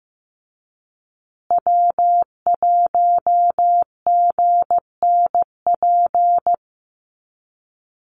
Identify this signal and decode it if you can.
Morse code, heard as a keyed tone, about 15 wpm, 700 Hz. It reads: W1GNP
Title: USA Call Signs Repeated 3x 15wpm